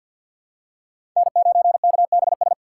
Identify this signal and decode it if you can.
Morse code, heard as a keyed tone, about 50 wpm, 700 Hz. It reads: N9KBS
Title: USA Call Signs - 2x Character Spacing - 50wpm